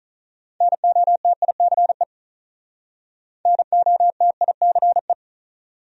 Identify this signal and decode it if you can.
Morse code, heard as a keyed tone, about 46 wpm, 700 Hz. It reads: NOTICE NOTICE